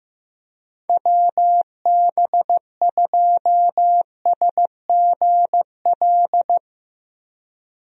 Morse code, heard as a keyed tone, about 15 wpm, 700 Hz. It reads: WB2SGL
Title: USA Call Signs - 6 Characters - 15wpm